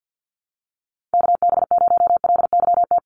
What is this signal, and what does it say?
Morse code, heard as a keyed tone, about 50 wpm, 700 Hz. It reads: KB0LYT
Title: USA Call Signs Repeated 3x 50wpm